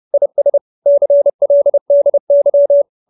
Morse code, heard as a keyed tone, about 30 wpm, 570 Hz. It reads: IS CLDY
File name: QSO310 30